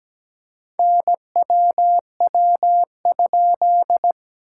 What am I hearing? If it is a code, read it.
Morse code, heard as a keyed tone, about 17 wpm, 700 Hz. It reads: NWW?
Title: SOTA QSO Practice 17wpm